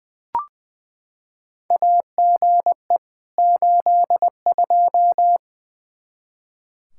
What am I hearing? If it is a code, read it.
Morse code, heard as a keyed tone, about 20 wpm, 700 Hz. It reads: AGE 82
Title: K1USN SST QSO Practice 20wpm